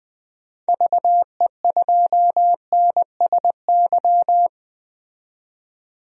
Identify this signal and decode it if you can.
Morse code, heard as a keyed tone, about 20 wpm, 700 Hz. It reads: VE2NSY
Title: DXCC-100 Call Signs - 6 Characters - 20wpm